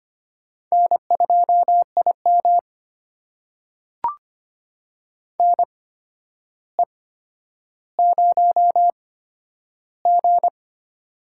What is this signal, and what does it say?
Morse code, heard as a keyed tone, about 25 wpm, spaced as for 11 wpm, 700 Hz. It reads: N2IM NE0G
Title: USA Call Signs - 4 Characters - 8x Character Spacing - 25wpm